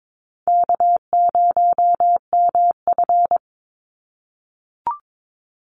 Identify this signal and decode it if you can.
Morse code, heard as a keyed tone, about 22 wpm, 700 Hz. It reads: K0MF
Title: USA Call Signs 22wpm